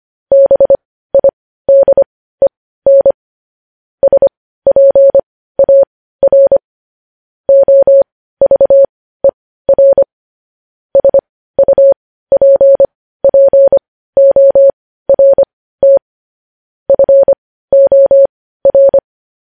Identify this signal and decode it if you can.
Morse code, heard as a keyed tone, about 25 wpm, spaced as for 15 wpm, 550 Hz. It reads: BIDEN SPAR OVER SUPPORT FOR
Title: News Headlines at 15